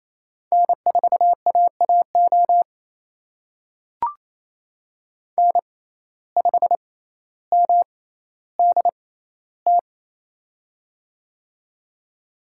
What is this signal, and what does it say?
Morse code, heard as a keyed tone, about 28 wpm, spaced as for 14 wpm, 700 Hz. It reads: N4AAO N5MDT
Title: USA Call Signs - 5 Characters - 6x Character Spacing - 28wpm